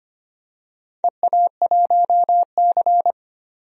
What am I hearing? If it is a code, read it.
Morse code, heard as a keyed tone, about 25 wpm, 700 Hz. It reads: EA1C
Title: DXCC-100 Call Signs - 4 Characters - 25wpm